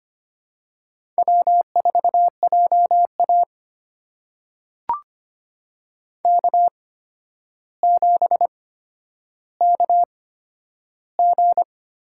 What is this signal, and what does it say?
Morse code, heard as a keyed tone, about 25 wpm, spaced as for 11 wpm, 700 Hz. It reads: W4JA K7KG
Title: USA Call Signs - 4 Characters - 8x Character Spacing - 25wpm